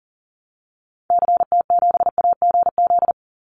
Morse code, heard as a keyed tone, about 40 wpm, 700 Hz. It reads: CT7AGZ
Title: Call Signs - 40wpm